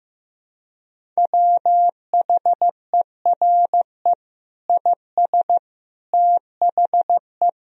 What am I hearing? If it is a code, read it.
Morse code, heard as a keyed tone, about 16 wpm, 700 Hz. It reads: WHERE IS THE
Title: Sentences from Top 1000 Words - Review Part 1 - 15wpm